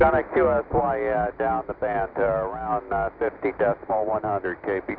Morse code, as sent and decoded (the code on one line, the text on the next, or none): none